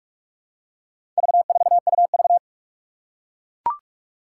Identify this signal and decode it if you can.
Morse code, heard as a keyed tone, about 44 wpm, 700 Hz. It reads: V4UV
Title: Four Challenging Letter-Number Combinations 45wpm